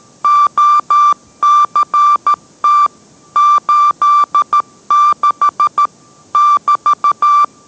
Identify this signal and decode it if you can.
Morse code, heard as a keyed tone, about 13 wpm, 1180 Hz. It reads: OCT 86 <BT>